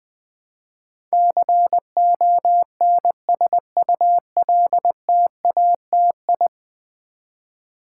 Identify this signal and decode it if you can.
Morse code, heard as a keyed tone, about 20 wpm, 700 Hz. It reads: CONSULTATI
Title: Build A Word - 5 Letters At A Time - 12-Letter Words - 20wpm